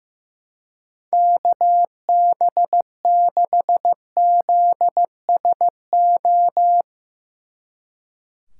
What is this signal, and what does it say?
Morse code, heard as a keyed tone, about 16 wpm, 700 Hz. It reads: KB6ZSO
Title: Speed Racing - USA 2x3 Call Signs 15wpm